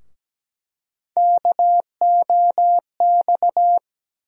Morse code, heard as a keyed tone, about 17 wpm, 700 Hz. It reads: KOX
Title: USA Call Sign Postfixes - 3 Letters 17wpm